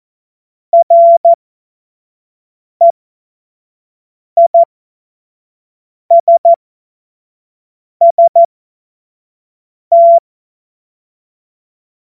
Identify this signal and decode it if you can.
Morse code, heard as a keyed tone, about 14 wpm, spaced as for 5 wpm, 680 Hz. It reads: REISST
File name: Morsekurs / Witze / Witz 12. Oktober 2021, 25 F70